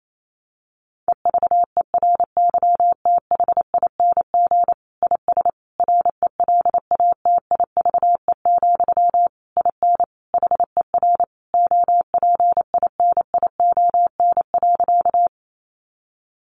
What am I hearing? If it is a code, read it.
Morse code, heard as a keyed tone, about 28 wpm, 700 Hz. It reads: EVERYTHING IS RELATIVE, IN HER OPINION.
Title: Sentences from Top 1400 Words 28wpm